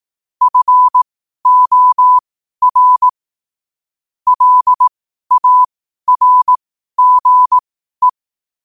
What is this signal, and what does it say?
Morse code, heard as a keyed tone, about 18 wpm, spaced as for 12 wpm, 1000 Hz. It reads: FOR LARGE
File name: qotdfiles / 12wpm